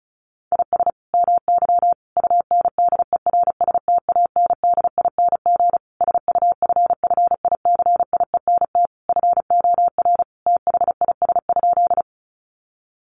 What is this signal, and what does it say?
Morse code, heard as a keyed tone, about 34 wpm, 700 Hz. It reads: IS MY UNDERSTANDING SUFFICIENT FOR THIS?
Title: Sentences from Top 1400 Words 35wpm